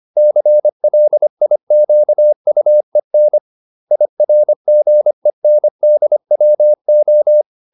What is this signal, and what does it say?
Morse code, heard as a keyed tone, about 25 wpm, 600 Hz. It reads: CLIQUEN IRGENDWO